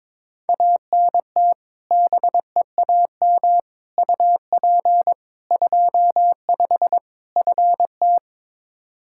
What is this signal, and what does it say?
Morse code, heard as a keyed tone, about 22 wpm, 700 Hz. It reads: ANT BEAM UP 25 FT
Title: K1USN SST QSO Practice 22wpm